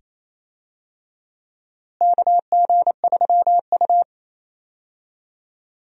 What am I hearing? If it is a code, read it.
Morse code, heard as a keyed tone, about 27 wpm, 700 Hz. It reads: KG3U
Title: USA Call Signs - 4 Characters - Repeated 2x - 28wpm